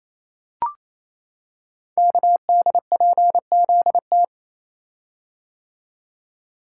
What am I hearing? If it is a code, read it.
Morse code, heard as a keyed tone, about 29 wpm, 700 Hz. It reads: KDPZT
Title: Code Groups - 5 Letters - 28wpm